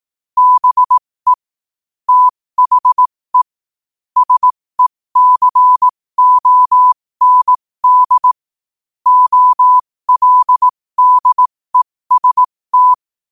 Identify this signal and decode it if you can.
Morse code, heard as a keyed tone, about 18 wpm, spaced as for 15 wpm, 1000 Hz. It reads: BE THE SECOND OLDEST
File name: qotdfiles / 15wpm